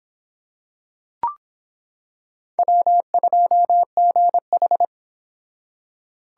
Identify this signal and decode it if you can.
Morse code, heard as a keyed tone, about 26 wpm, 700 Hz. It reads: W2GH